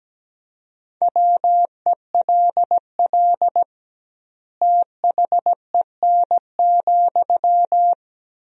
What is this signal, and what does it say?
Morse code, heard as a keyed tone, about 17 wpm, 700 Hz. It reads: WELL THEN,
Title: 2x Word-Spacing - Sentences from Top 200 Words - 17wpm